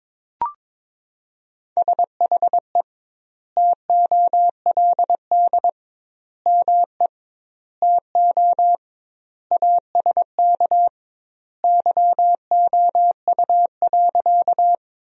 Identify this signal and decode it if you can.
Morse code, heard as a keyed tone, about 22 wpm, 700 Hz. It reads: SHE TOLD ME TO ASK YOU.
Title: 2x Word-Spacing - Sentences from Top 400 Words - 22wpm